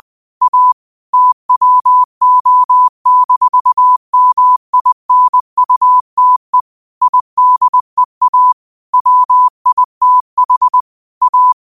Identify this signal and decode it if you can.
Morse code, heard as a keyed tone, about 20 wpm, 1000 Hz. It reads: A TWO-MINUTE IDEA WITH A
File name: qotdfiles / 20wpm